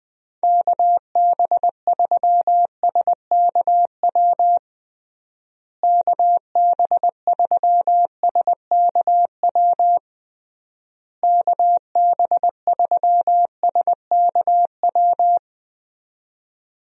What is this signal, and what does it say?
Morse code, heard as a keyed tone, about 21 wpm, 700 Hz. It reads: KB3SKW KB3SKW KB3SKW
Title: USA Call Signs - 6 Characters - Repeated 3x - 20wpm